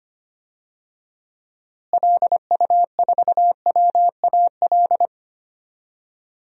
Morse code, heard as a keyed tone, about 25 wpm, 700 Hz. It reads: LU4WAL
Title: DXCC-100 Call Signs - 6 Characters - Repeated 3x - 25wpm